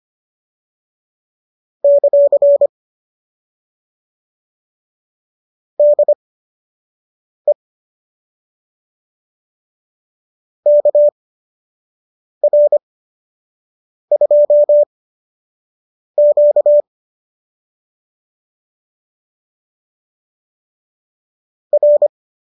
Morse code, heard as a keyed tone, about 25 wpm, spaced as for 6 wpm, 590 Hz. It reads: ; DE KR2Q R